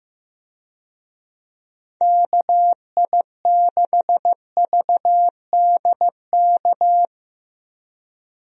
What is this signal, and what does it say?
Morse code, heard as a keyed tone, about 15 wpm, 700 Hz. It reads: KI6VDK
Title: USA Call Signs - 6 Characters - Repeated 2x - 15wpm